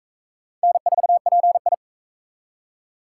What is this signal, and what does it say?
Morse code, heard as a keyed tone, about 37 wpm, 700 Hz. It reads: N4PI